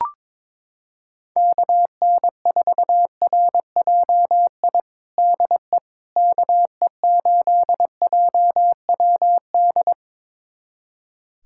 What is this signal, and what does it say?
Morse code, heard as a keyed tone, about 22 wpm, 700 Hz. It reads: KN4RJI DE KE8JWD
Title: USA Callsign DE Callsign - 6 Characters - 22wpm